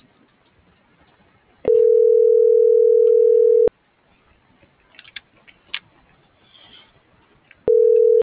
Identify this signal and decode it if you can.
Ringback tone (North America), 440 + 480 Hz.